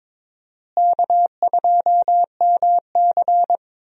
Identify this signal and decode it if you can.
Morse code, heard as a keyed tone, about 22 wpm, 700 Hz. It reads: K2MC
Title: USA Call Signs - 4 Characters - Repeated 2x - 22wpm